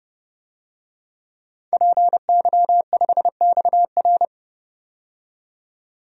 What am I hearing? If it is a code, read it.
Morse code, heard as a keyed tone, about 30 wpm, 700 Hz. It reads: PY5XR
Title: DXCC-100 Call Signs - 5 Characters - 30wpm